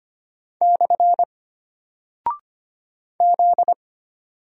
Morse code, heard as a keyed tone, about 25 wpm, 700 Hz. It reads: / Z
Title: Single Character (Letter:Number:Punctuation:Prosign) Alt 25wpm